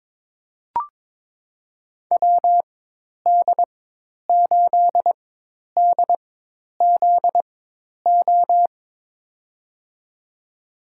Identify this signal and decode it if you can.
Morse code, heard as a keyed tone, about 22 wpm, spaced as for 10 wpm, 700 Hz. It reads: WD8DZO
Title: USA Call Signs - 4x Character Spacing - 22wpm